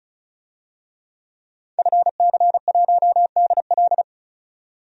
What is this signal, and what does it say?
Morse code, heard as a keyed tone, about 35 wpm, 700 Hz. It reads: FC1DL
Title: DXCC-100 Call Signs - 5 Characters - 35wpm